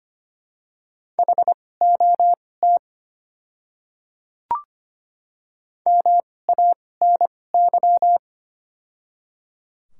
Morse code, heard as a keyed tone, about 25 wpm, spaced as for 18 wpm, 690 Hz. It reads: HOT MANY
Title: 2x Character Spacing - Top 100 Words - 25wpm